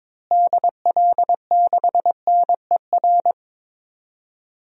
Morse code, heard as a keyed tone, about 22 wpm, 700 Hz. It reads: DL6NER
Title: DXCC-100 Call Signs - 6 Characters - Repeated 3x - 22wpm